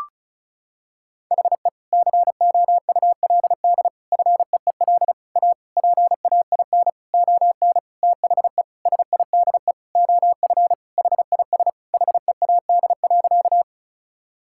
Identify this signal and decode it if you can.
Morse code, heard as a keyed tone, about 34 wpm, 700 Hz. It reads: HE COULD FEEL A PAIN ON THE SIDE OF HIS HEAD.
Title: Sentences from Top 1600 Words 35wpm